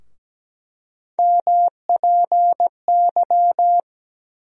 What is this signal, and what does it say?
Morse code, heard as a keyed tone, about 17 wpm, 700 Hz. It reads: MPY
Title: USA Call Sign Postfixes - 3 Letters 17wpm